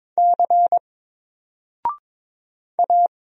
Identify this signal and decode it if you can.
Morse code, heard as a keyed tone, about 22 wpm, 700 Hz. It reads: C A
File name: Single-Letter-Number-22wpm